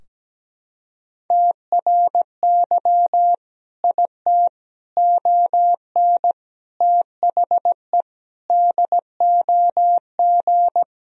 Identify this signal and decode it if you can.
Morse code, heard as a keyed tone, about 17 wpm, 700 Hz. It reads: TRY IT ON THE DOG
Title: Sentences from Top 1000 Words - Review Part 2 - 17wpm